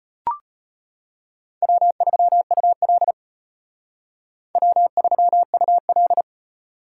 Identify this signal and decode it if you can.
Morse code, heard as a keyed tone, about 32 wpm, 700 Hz. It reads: W3UL W3UL